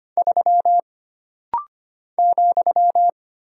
Morse code, heard as a keyed tone, about 25 wpm, 700 Hz. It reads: 3 ,
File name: Single Character (Letter:Number:Punctuation:Prosign) - Rapid-Fire - 25wpm